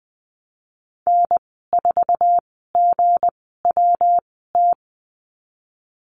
Morse code, heard as a keyed tone, about 20 wpm, spaced as for 15 wpm, 700 Hz. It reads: N4GWT
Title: 186 Lesson 30g - Letter b - Call Signs 20wpm